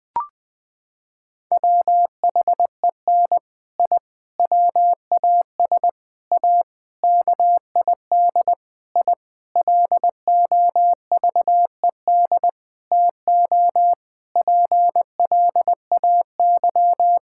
Morse code, heard as a keyed tone, about 20 wpm, 700 Hz. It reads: WHEN I WAS A KID I LOVED TO PLAY
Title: Sentences with 5-Word Phrases - 20wpm